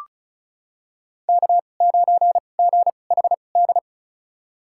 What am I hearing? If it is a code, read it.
Morse code, heard as a keyed tone, about 36 wpm, spaced as for 25 wpm, 700 Hz. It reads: K9GHD
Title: USA Call Signs - 5 Characters - 2x Character Spacing - 35wpm